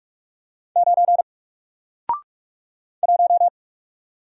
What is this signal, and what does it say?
Morse code, heard as a keyed tone, about 44 wpm, 700 Hz. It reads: 9 1
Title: Single Letter-Number Alt 45wpm